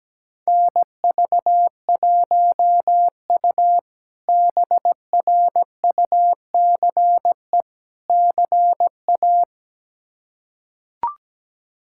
Morse code, heard as a keyed tone, about 17 wpm, 700 Hz. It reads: NV1U BRUCE CA
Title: K1USN SST QSO Practice - 17wpm